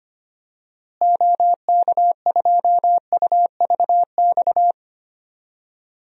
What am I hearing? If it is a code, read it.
Morse code, heard as a keyed tone, about 25 wpm, 700 Hz. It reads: OK2UVX
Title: DXCC-100 Call Signs - 6 Characters - Repeated 3x - 25wpm